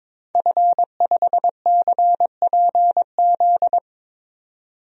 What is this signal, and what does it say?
Morse code, heard as a keyed tone, about 22 wpm, 700 Hz. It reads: F5CPZ
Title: DXCC-100 Call Signs - 5 Characters - 22wpm